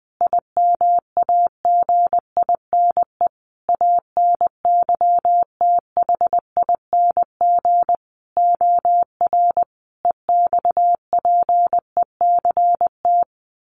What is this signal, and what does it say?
Morse code, heard as a keyed tone, about 19 wpm, 700 Hz. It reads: IMAGINE ANYTHING OR EXPECT
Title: Sentences from Top 1400 Words 20wpm